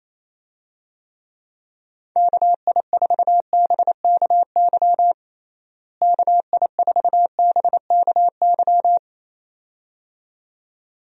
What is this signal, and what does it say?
Morse code, heard as a keyed tone, about 28 wpm, 700 Hz. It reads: KI4BKY KI4BKY
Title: USA Call Signs - 6 Characters - Repeated 2x - 28wpm